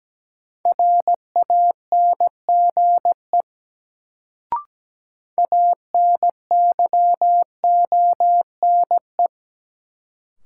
Morse code, heard as a keyed tone, about 17 wpm, 700 Hz. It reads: RANGE ANYONE
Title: Sentences from Top 900 Words - New Word Drill - 17wpm